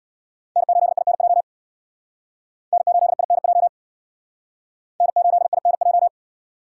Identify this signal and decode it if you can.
Morse code, heard as a keyed tone, about 61 wpm, 700 Hz. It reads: N8INJ N8INJ N8INJ
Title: Speed Racing - USA Call Signs 50wpm